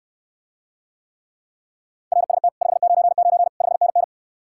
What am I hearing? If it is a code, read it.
Morse code, heard as a keyed tone, about 69 wpm, 700 Hz. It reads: RST 599 5NN